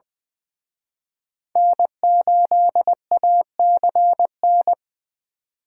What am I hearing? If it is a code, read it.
Morse code, heard as a keyed tone, about 23 wpm, 700 Hz. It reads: N8ACN
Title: Speed Racing - USA 1x3 Call Signs 20wpm